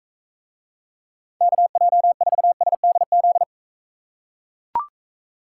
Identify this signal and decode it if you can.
Morse code, heard as a keyed tone, about 40 wpm, 700 Hz. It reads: KJ4SDZ